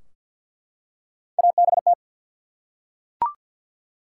Morse code, heard as a keyed tone, about 50 wpm, 700 Hz. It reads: ABT
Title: Top 90 QSO Elements 50wpm